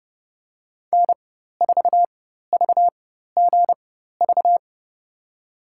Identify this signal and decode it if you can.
Morse code, heard as a keyed tone, about 31 wpm, spaced as for 14 wpm, 700 Hz. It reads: N4VGV
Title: USA Call Signs - 5 Characters - 4x Character Spacing - 30wpm